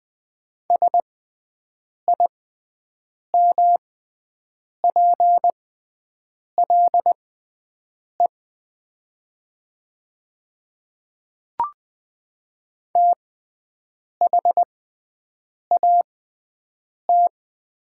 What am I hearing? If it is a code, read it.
Morse code, heard as a keyed tone, about 20 wpm, spaced as for 7 wpm, 700 Hz. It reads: SIMPLE THAT
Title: 6x Character Spacing - Top 500 Words - 20wpm